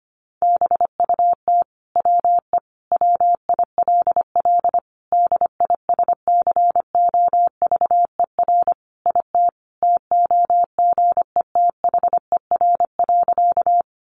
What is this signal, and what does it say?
Morse code, heard as a keyed tone, about 25 wpm, 700 Hz. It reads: BUT WE WILL DISCOVER IT TOGETHER.
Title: Sentences from Top 1100 Words 25wpm